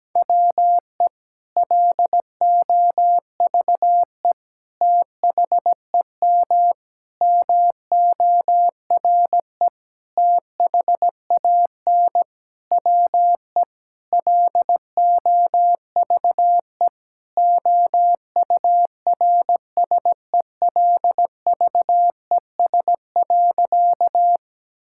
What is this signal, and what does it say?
Morse code, heard as a keyed tone, about 17 wpm, 700 Hz. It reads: WE LOVE THEM MORE THAN WE LOVE OURSELVES.
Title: Sentences from Top 1600 Words 17wpm